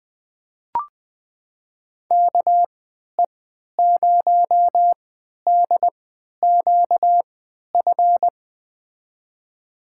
Morse code, heard as a keyed tone, about 21 wpm, spaced as for 11 wpm, 700 Hz. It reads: KE0DQF
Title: USA Call Signs - 6 Characters - 3x Character Spacing - 20wpm